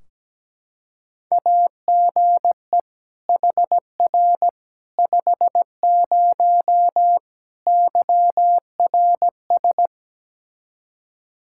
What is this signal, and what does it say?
Morse code, heard as a keyed tone, about 17 wpm, 700 Hz. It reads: AGE HR 50 YRS
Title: K1USN SST QSO Practice 17wpm